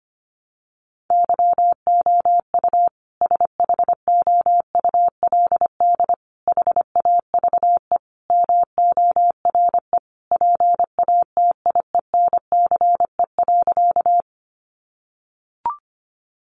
Morse code, heard as a keyed tone, about 25 wpm, 700 Hz. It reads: YOU SHOULD HAVE MORE PATIENCE.
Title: Sentences from Top 1300 Words 25wpm